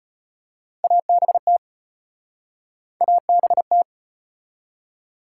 Morse code, heard as a keyed tone, about 33 wpm, 700 Hz. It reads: ABT ABT